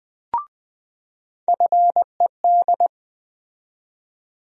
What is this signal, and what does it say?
Morse code, heard as a keyed tone, about 20 wpm, 700 Hz. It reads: FED
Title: USA Call Sign Postfixes - 3 Letters 20wpm